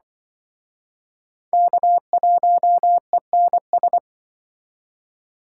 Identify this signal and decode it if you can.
Morse code, heard as a keyed tone, about 24 wpm, 700 Hz. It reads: K1ENS
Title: Speed Racing - USA 1x3 Call Signs 20wpm